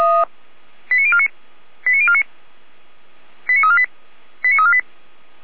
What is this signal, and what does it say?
Five-tone selective call (ZVEI), heard as quick runs of five tones, tones of about 70 ms. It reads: 89049 89049 89348 89348
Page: /455/SDR/ZVEI/